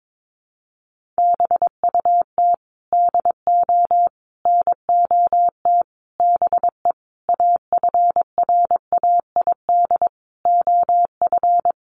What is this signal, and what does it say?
Morse code, heard as a keyed tone, about 22 wpm, 700 Hz. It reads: BUT DO NOT BE AFRAID OF
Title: Sentences from Top 1500 Words 22wpm